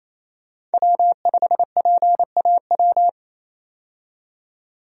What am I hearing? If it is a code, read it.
Morse code, heard as a keyed tone, about 28 wpm, 700 Hz. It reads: W5PAW
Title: USA Call Signs - 5 Characters - 28wpm